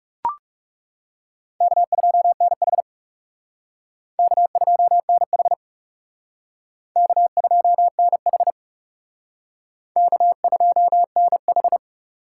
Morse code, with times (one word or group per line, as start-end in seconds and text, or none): 1.6-2.8 K2NH
4.2-5.5 K2NH
7.0-8.5 K2NH
10.0-11.8 K2NH